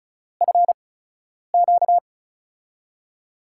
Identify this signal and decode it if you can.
Morse code, heard as a keyed tone, about 36 wpm, spaced as for 10 wpm, 700 Hz. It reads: FQ
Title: USA Call Signs - 5 Characters - 8x Character Spacing - 35wpm